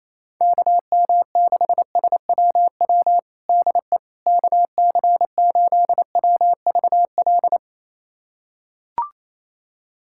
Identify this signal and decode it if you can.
Morse code, heard as a keyed tone, about 28 wpm, 700 Hz. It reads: KM6SWW DE KC8WVL
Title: USA Callsign DE Callsign 28wpm